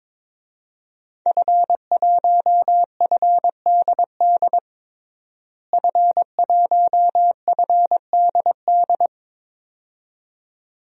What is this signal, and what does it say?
Morse code, heard as a keyed tone, about 22 wpm, 700 Hz. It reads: F1FDD F1FDD
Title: DXCC-100 Call Signs - 5 Characters - Repeated 2x - 22wpm